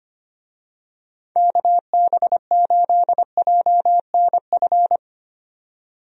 Morse code, heard as a keyed tone, about 25 wpm, 700 Hz. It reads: KB8JNF
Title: USA Call Signs - 6 Characters - 25wpm